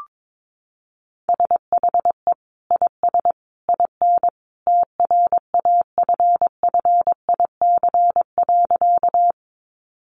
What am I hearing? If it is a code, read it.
Morse code, heard as a keyed tone, about 22 wpm, 700 Hz. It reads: SHE IS IN TRAFFIC.
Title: Sentences from Top 1300 Words 22wpm